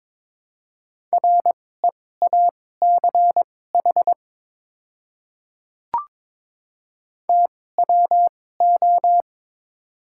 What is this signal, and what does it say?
Morse code, heard as a keyed tone, about 21 wpm, spaced as for 16 wpm, 690 Hz. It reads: REACH TWO
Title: 2x Character Spacing - Top 500 Words - 22wpm